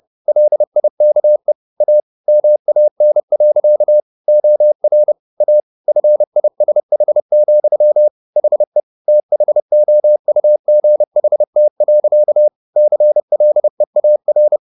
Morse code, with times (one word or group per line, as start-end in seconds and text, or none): 0.3-1.5 LIKE
1.8-2.0 A
2.3-4.0 MAN.
4.3-5.1 OR
5.4-5.6 A
5.9-8.1 FISH,
8.4-8.8 HE
9.1-12.5 THOUGHT.
12.8-14.6 CLEAR